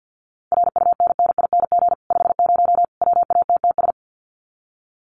Morse code, heard as a keyed tone, about 50 wpm, 700 Hz. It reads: RUNNING 50 WATTS
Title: Common QSO Phrases 50wpm